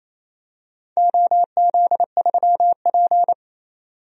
Morse code, heard as a keyed tone, about 28 wpm, 700 Hz. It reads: OZ3P
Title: DXCC-100 Call Signs - 4 Characters - 28wpm